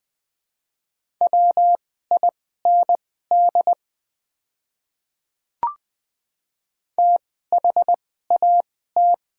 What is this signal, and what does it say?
Morse code, heard as a keyed tone, about 20 wpm, spaced as for 14 wpm, 700 Hz. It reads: WIND THAT
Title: 2x Character Spacing - Top 500 Words - 20wpm